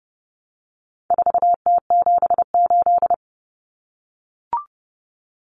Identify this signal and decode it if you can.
Morse code, heard as a keyed tone, about 30 wpm, 700 Hz. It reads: 4T78
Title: Common Cut Numbers - 4-Digit 30wpm